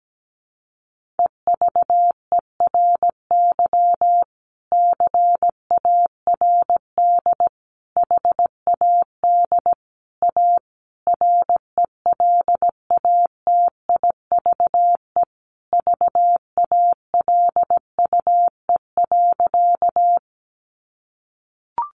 Morse code, heard as a keyed tone, about 17 wpm, 700 Hz. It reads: EVERY CARD HAD A RELATIVE VALUE.
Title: Sentences from Top 1400 Words 17wpm